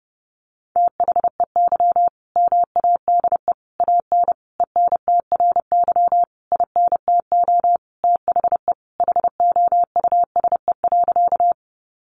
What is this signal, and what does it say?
Morse code, heard as a keyed tone, about 30 wpm, 700 Hz. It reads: THEY MADE AN ENTRY INTO THE HOUSE.
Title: Sentences from Top 1500 Words 30wpm